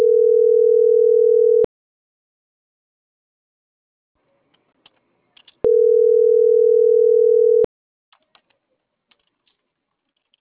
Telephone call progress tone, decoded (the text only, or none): ringback tone